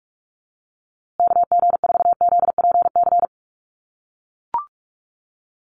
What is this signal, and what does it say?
Morse code, heard as a keyed tone, about 44 wpm, 690 Hz. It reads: KG4ZPC
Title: USA Call Signs Repeated 2x 45wpm